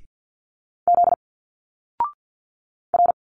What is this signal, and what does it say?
Morse code, heard as a keyed tone, about 47 wpm, 700 Hz. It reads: Z R
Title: Single Letters 50wpm